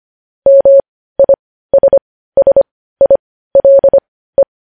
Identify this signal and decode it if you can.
Morse code, heard as a keyed tone, about 25 wpm, spaced as for 15 wpm, 550 Hz. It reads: MISSILE
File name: podcast / News headlines-1x-25x15